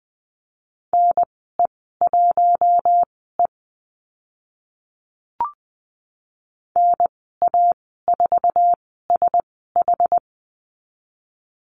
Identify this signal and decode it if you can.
Morse code, heard as a keyed tone, about 20 wpm, spaced as for 14 wpm, 700 Hz. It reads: NE1E NA4SH
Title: 069 Lesson 12g - Letter d - Call Signs 20wpm